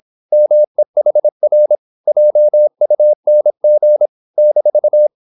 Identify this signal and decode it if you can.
Morse code, heard as a keyed tone, about 26 wpm, 600 Hz. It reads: MEHR JUNG -